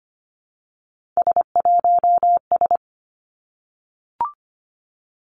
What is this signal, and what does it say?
Morse code, heard as a keyed tone, about 25 wpm, 700 Hz. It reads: S1S
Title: Three Challenging Letter-Number Combinations 25wpm